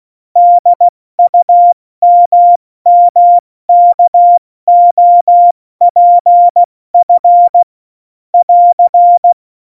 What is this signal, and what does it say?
Morse code, heard as a keyed tone, about 16 wpm, spaced as for 14 wpm, 700 Hz. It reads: DUMMKOPF <AR>
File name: Morsekurs / Witze / Witz 11. September 2018, 70 F80